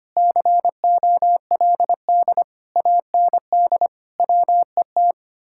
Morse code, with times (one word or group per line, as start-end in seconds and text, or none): 0.2-2.4 COLD
2.8-3.9 AND
4.2-5.1 WET